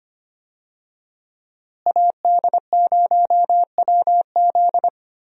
Morse code, heard as a keyed tone, about 25 wpm, 700 Hz. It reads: AD0WZ
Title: USA Call Signs - 5 Characters - Repeated 2x - 25wpm